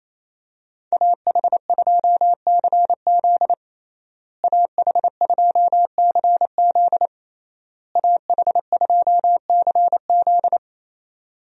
Morse code, heard as a keyed tone, about 28 wpm, 700 Hz. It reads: AH2CZ AH2CZ AH2CZ
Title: USA Call Signs - 5 Characters - Repeated 3x - 28wpm